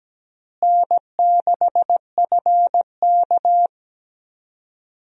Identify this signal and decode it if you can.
Morse code, heard as a keyed tone, about 17 wpm, 700 Hz. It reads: N6FK
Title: USA Call Signs - 4 Characters - Repeated 3x - 17wpm